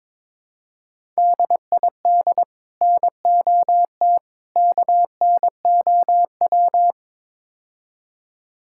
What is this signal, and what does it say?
Morse code, heard as a keyed tone, about 22 wpm, 700 Hz. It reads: DID NOT KNOW
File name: practice / 22wpm / Build A Phrase - 5-Word Phrases - 22wpm